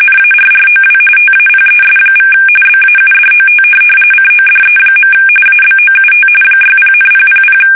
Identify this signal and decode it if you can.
RTTY, about 45.45 baud. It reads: NARROW TAPES= BAUDOT CODE. ABOUT 60,000
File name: fsk850 / teletype-corp